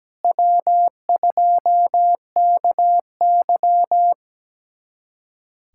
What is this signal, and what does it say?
Morse code, heard as a keyed tone, about 17 wpm, 700 Hz. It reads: W2KY
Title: USA Callsign DE Callsign - 4 Characters - 17wpm